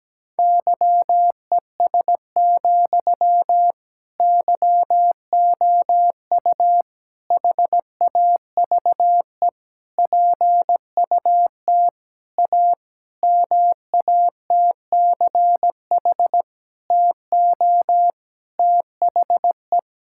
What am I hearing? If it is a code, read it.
Morse code, heard as a keyed tone, about 17 wpm, 700 Hz. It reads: YES, YOU HAVE PUT A MATCH TO THE
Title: Sentences from Top 1600 Words 17wpm